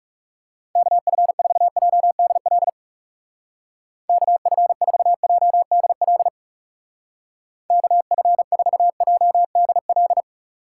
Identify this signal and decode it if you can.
Morse code, heard as a keyed tone, about 38 wpm, 700 Hz. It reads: KF4JDL KF4JDL KF4JDL